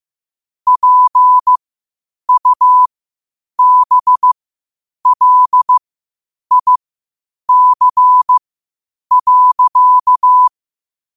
Morse code, heard as a keyed tone, about 15 wpm, spaced as for 8 wpm, 1000 Hz. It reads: PUBLIC.
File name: qotdfiles / 8wpm